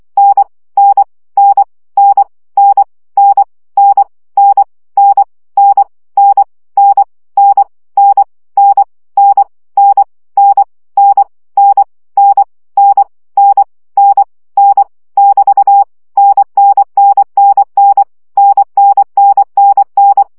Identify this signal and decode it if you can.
Morse code, heard as a keyed tone, about 24 wpm, 800 Hz. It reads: N N N N N N N N N N N N N N N N N N N N N N N N N <BT> NNNNN NNNNN
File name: CW gyakorlas n 20 baud